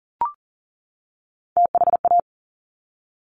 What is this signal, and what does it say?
Morse code, heard as a keyed tone, about 41 wpm, 690 Hz. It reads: THA
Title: Top Three Letter Combinations in Words 40wpm